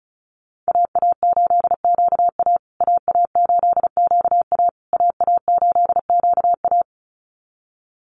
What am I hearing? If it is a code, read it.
Morse code, heard as a keyed tone, about 34 wpm, 700 Hz. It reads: AA8QA AA8QA AA8QA
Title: USA Call Signs Repeated 3x 35wpm